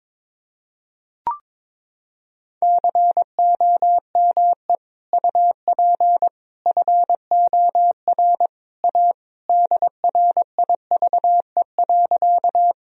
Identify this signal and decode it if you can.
Morse code, heard as a keyed tone, about 22 wpm, 700 Hz. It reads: COME UP FOR A DRIVE.
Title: Sentences from Top 1000 Words - Review Part 2 - 22wpm